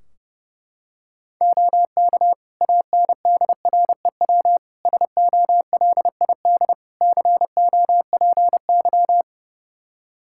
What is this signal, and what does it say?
Morse code, heard as a keyed tone, about 32 wpm, 700 Hz. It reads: OK ANDREW SOLID COPY
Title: Practice QSO - Exchange 3 - More Info - 30wpm